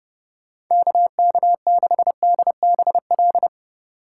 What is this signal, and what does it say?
Morse code, heard as a keyed tone, about 31 wpm, 700 Hz. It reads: KK6DBL